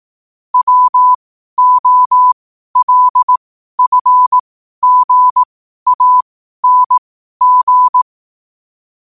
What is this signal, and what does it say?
Morse code, heard as a keyed tone, about 18 wpm, spaced as for 12 wpm, 1000 Hz. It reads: WOLFGANG
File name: qotdfiles / 12wpm